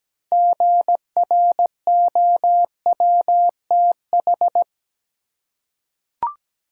Morse code, heard as a keyed tone, about 17 wpm, 700 Hz. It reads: GROWTH
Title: Six Letter Words - 17wpm